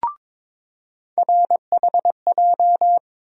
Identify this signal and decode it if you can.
Morse code, heard as a keyed tone, about 22 wpm, 700 Hz. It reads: RHJ
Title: USA Call Sign Postfixes - 3 Letters 22wpm